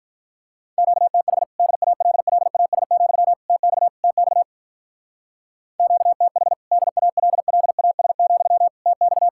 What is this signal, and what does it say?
Morse code, heard as a keyed tone, about 45 wpm, 700 Hz. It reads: QTH DALLAS, TX TX QTH DALLAS, TX